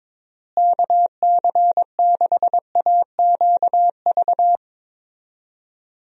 Morse code, heard as a keyed tone, about 22 wpm, 700 Hz. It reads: KC6AQV
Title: USA Call Signs - 6 Characters - Repeated 2x - 22wpm